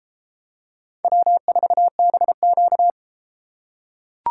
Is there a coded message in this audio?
Yes: Morse code from 1.0 to 2.9 s, heard as a keyed tone, about 30 wpm, 700 Hz. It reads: W4BQ